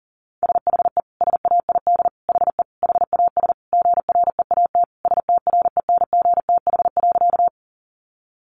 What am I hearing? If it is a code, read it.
Morse code, heard as a keyed tone, about 40 wpm, 700 Hz. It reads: SHE SAID HE HAS GREAT STRENGTH.
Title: Sentences from Top 1300 Words 40wpm